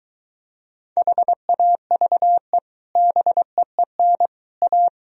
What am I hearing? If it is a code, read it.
Morse code, heard as a keyed tone, about 23 wpm, 700 Hz. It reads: HAVE BEEN A